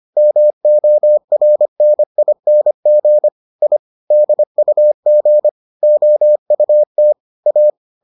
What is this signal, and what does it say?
Morse code, heard as a keyed tone, about 25 wpm, 600 Hz. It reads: MORNING I DUG OUT A